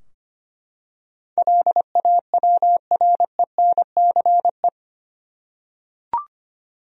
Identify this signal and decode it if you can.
Morse code, heard as a keyed tone, about 25 wpm, 700 Hz. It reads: LAWRENCE
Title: USA Common Names - 25wpm